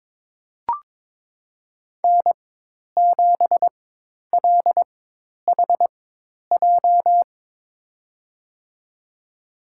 Morse code, heard as a keyed tone, about 22 wpm, spaced as for 10 wpm, 700 Hz. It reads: N7LHJ
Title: USA Call Signs - 4x Character Spacing - 22wpm